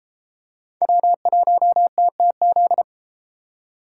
Morse code, heard as a keyed tone, about 32 wpm, 700 Hz. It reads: W1TTZ